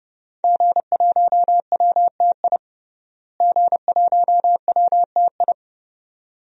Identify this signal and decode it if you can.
Morse code, heard as a keyed tone, about 31 wpm, 700 Hz. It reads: G1WTI G1WTI
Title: DXCC-100 Call Signs - 5 Characters - Repeated 2x - 30wpm